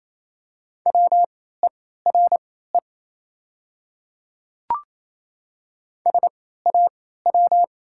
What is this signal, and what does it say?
Morse code, heard as a keyed tone, about 28 wpm, spaced as for 16 wpm, 700 Hz. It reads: WERE SAW
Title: 3x Character Spacing - Top 300 Words - 28wpm